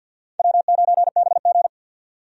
Morse code, heard as a keyed tone, about 53 wpm, 700 Hz. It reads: W9BG
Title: USA Call Signs - 4 Characters - 50wpm